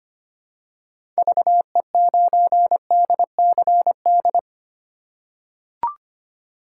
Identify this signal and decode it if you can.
Morse code, heard as a keyed tone, about 25 wpm, 700 Hz. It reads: VE9DCD
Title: K1USN SST QSO Practice - 25wpm